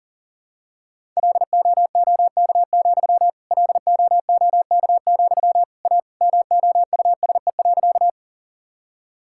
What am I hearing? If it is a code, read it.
Morse code, heard as a keyed tone, about 37 wpm, 700 Hz. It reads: LOOK, LOOK, A MOUSE.
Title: Sentences from Top 1000 Words - Review Part 5 - 40wpm